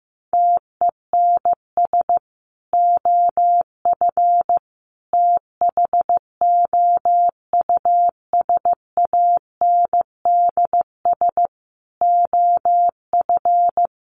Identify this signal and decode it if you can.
Morse code, heard as a keyed tone, about 14 wpm, 700 Hz. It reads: TENS OF THOUSANDS OF
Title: Sets of 5 Words Encore2 15wpm